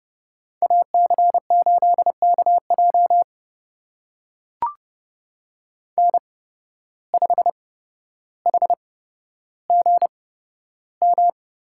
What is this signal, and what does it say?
Morse code, heard as a keyed tone, about 32 wpm, spaced as for 13 wpm, 700 Hz. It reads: AC8KJ N5HGM
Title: USA Call Signs - 5 Characters - 8x Character Spacing - 30wpm